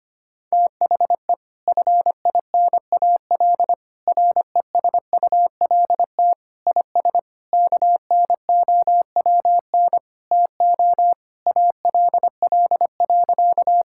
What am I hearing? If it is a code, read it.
Morse code, heard as a keyed tone, about 25 wpm, 700 Hz. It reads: THE FINAL RESULT IS KNOWN TO ALL.
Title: Sentences from Top 1000 Words - Review Part 5 - 25wpm